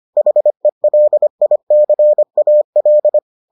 Morse code, heard as a keyed tone, about 25 wpm, 600 Hz. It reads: HELICAL